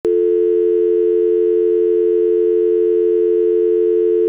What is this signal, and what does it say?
Dial tone (North America), 350 + 440 Hz.
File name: Telephone Dial Tone